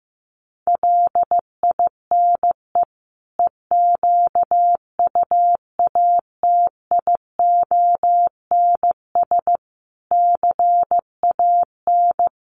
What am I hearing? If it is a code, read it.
Morse code, heard as a keyed tone, about 15 wpm, 700 Hz. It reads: LINE EQUATIONS CAN
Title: Sets of 5 Words 15wpm